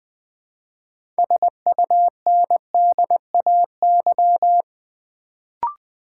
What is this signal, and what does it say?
Morse code, heard as a keyed tone, about 20 wpm, 700 Hz. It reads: SUNDAY
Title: Build A Word - 2 Letters At A Time - 6-Letter Words - 20wpm